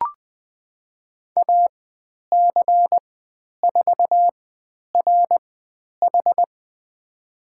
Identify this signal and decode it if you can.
Morse code, heard as a keyed tone, about 20 wpm, spaced as for 10 wpm, 700 Hz. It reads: AC4RH
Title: 102 Lesson 17g - Letter u - Call Signs 20wpm